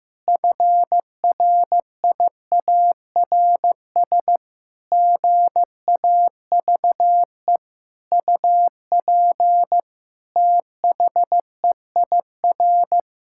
Morse code, heard as a keyed tone, about 15 wpm, 700 Hz. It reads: FRIARS GAVE UP THEIR
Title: Pangram Challenge v2 - 15wpm